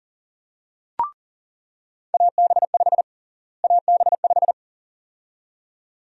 Morse code, heard as a keyed tone, about 40 wpm, 700 Hz. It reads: AB5 AB5
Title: USA Call Sign Prefixes - Repeated 2x - 40wpm